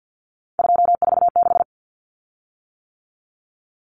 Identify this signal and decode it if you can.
Morse code, heard as a keyed tone, about 48 wpm, 700 Hz. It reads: 246
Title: Common Cut Numbers - 3-Digit 50wpm